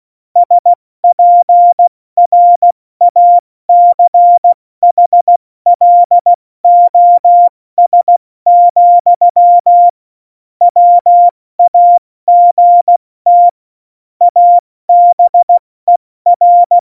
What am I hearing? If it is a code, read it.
Morse code, heard as a keyed tone, about 16 wpm, 700 Hz. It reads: SPRACHLOS, WAGT ABER
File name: Morsekurs / Witze / Witz 16. April 2019, 70 F80